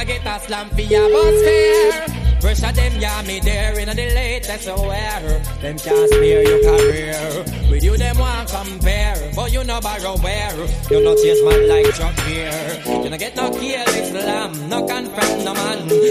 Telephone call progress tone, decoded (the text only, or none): ringback tone